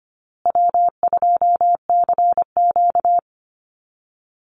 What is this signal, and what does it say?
Morse code, heard as a keyed tone, about 25 wpm, 700 Hz. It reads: W2CQ
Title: Easy Call Signs Repeated 2x - 25wpm